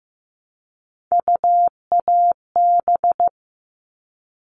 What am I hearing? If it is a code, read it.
Morse code, heard as a keyed tone, about 15 wpm, 700 Hz. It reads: UAB
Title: All Cut Numbers - 3-Digit 15wpm